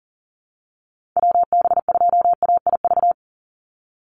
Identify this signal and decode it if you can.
Morse code, heard as a keyed tone, about 40 wpm, 700 Hz. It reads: WB2AIV
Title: Call Signs - 40wpm